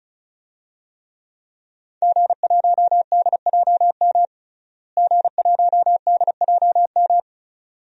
Morse code, heard as a keyed tone, about 34 wpm, 700 Hz. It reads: G1DJM G1DJM
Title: DXCC-100 Call Signs - 5 Characters - Repeated 2x - 35wpm